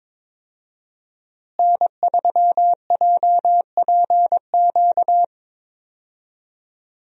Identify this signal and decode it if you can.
Morse code, heard as a keyed tone, about 21 wpm, 700 Hz. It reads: N3JPQ
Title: USA Call Signs - 5 Characters - Repeated 2x - 22wpm